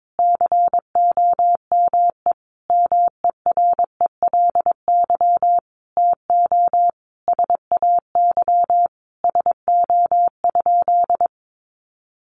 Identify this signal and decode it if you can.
Morse code, heard as a keyed tone, about 22 wpm, 700 Hz. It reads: COME MERELY TO SAY SO?
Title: Sentences from Top 1400 Words 22wpm